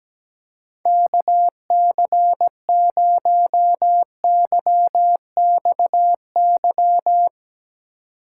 Morse code, heard as a keyed tone, about 17 wpm, 700 Hz. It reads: KC0YXY
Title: Speed Racing - USA Call Signs 17wpm